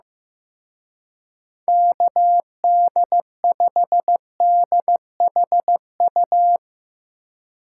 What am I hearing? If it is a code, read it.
Morse code, heard as a keyed tone, about 15 wpm, 700 Hz. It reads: KD5DHU
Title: Speed Racing - USA Call Signs 15wpm